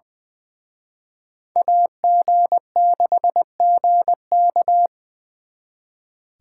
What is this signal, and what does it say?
Morse code, heard as a keyed tone, about 24 wpm, 700 Hz. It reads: AG6GK
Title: Speed Racing - USA 2x2 Call Signs 20wpm